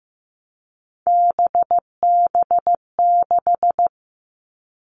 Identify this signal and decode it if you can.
Morse code, heard as a keyed tone, about 15 wpm, 700 Hz. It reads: BB6
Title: All Cut Numbers - 3-Digit 15wpm